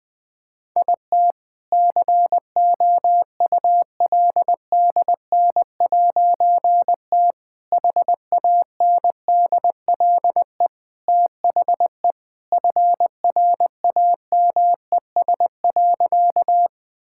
Morse code, heard as a keyed tone, about 20 wpm, 700 Hz. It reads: IT COULDN'T HANDLE THE FRAMES.
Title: All-Age Appropriate Jokes - 20wpm